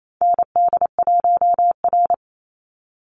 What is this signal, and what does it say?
Morse code, heard as a keyed tone, about 28 wpm, 700 Hz. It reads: ND1R
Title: Call Signs Repeated 2x - 28wpm